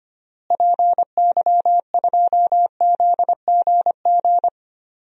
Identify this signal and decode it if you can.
Morse code, heard as a keyed tone, about 25 wpm, 700 Hz. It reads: PY2ZGG
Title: DXCC-100 Call Signs - 6 Characters - Repeated 2x - 25wpm